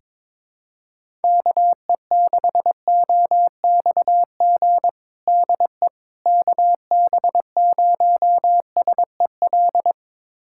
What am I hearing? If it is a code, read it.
Morse code, heard as a keyed tone, about 22 wpm, 700 Hz. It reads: KE6OXG DE KB0SEL
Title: USA Callsign DE Callsign - 6 Characters - 22wpm